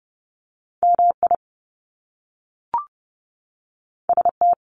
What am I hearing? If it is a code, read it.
Morse code, heard as a keyed tone, about 30 wpm, 700 Hz. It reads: MI ST
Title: Top Two Letter Combinations in Words 30wpm